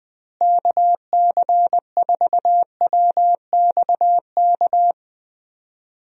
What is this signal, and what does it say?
Morse code, heard as a keyed tone, about 21 wpm, 700 Hz. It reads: KC4WXK
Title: USA Call Signs - 6 Characters - Repeated 3x - 20wpm